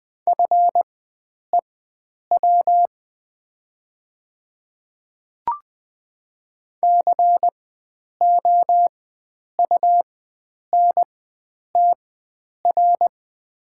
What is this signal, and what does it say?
Morse code, heard as a keyed tone, about 20 wpm, spaced as for 9 wpm, 690 Hz. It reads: FEW COUNTR
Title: 4x Character Spacing - Top 300 Words - 20wpm